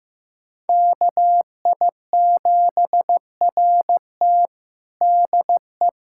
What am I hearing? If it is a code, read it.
Morse code, heard as a keyed tone, about 15 wpm, 700 Hz. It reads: KI7RT DE
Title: USA Callsign DE Callsign - 5 Characters - 15wpm